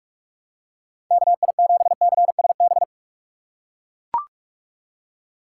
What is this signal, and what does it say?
Morse code, heard as a keyed tone, about 44 wpm, 700 Hz. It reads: KI7CSB
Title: USA Call Signs - 6 Characters - 2x Character Spacing - 45wpm